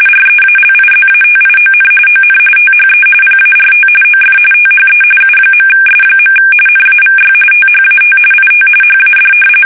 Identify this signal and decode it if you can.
RTTY, about 45.45 baud. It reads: REALIZATION ON THE PART OF THE U.S. NAVY THAT IT HAD
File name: fsk850 / usnavy3